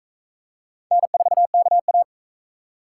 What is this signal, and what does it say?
Morse code, heard as a keyed tone, about 42 wpm, 700 Hz. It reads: N4KA